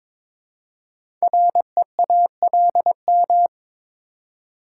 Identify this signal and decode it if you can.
Morse code, heard as a keyed tone, about 22 wpm, 700 Hz. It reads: REALM
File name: 5 Letter Words - First 3 Letters Given - 22wpm